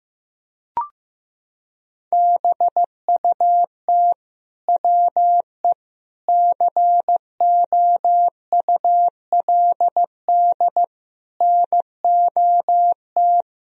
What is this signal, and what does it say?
Morse code, heard as a keyed tone, about 16 wpm, 700 Hz. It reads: BUT WE COULD NOT
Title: Sentences from Top 1000 Words - Review Part 2 - 15wpm